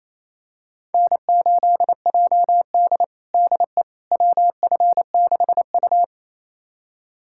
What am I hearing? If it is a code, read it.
Morse code, heard as a keyed tone, about 28 wpm, 700 Hz. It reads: N8JD DE WF6U
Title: USA Callsign DE Callsign - 4 Characters - 28wpm